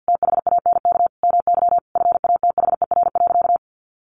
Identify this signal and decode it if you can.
Morse code, heard as a keyed tone, about 50 wpm, 700 Hz. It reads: THANK MY FATHER.
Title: Sentences from Top 1100 Words 50wpm